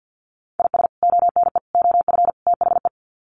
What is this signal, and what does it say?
Morse code, heard as a keyed tone, about 50 wpm, 700 Hz. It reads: IS ONE OF THE
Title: Sets of 4 Words 50wpm